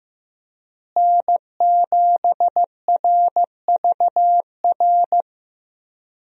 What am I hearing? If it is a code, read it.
Morse code, heard as a keyed tone, about 15 wpm, 700 Hz. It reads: N7RVR
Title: USA Call Signs - 5 Characters - Repeated 2x - 15wpm